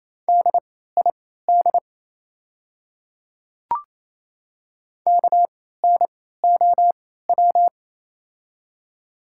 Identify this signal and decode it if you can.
Morse code, heard as a keyed tone, about 28 wpm, spaced as for 16 wpm, 700 Hz. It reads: DID KNOW
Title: 3x Character Spacing - Top 100 Words - 28wpm